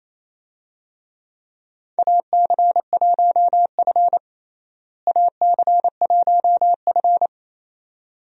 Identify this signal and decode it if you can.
Morse code, heard as a keyed tone, about 27 wpm, 700 Hz. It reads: AC1F AC1F
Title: USA Call Signs - 4 Characters - Repeated 2x - 28wpm